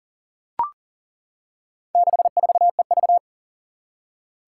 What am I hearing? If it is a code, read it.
Morse code, heard as a keyed tone, about 40 wpm, 700 Hz. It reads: 64EV
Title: Four Challenging Letter-Number Combinations 40wpm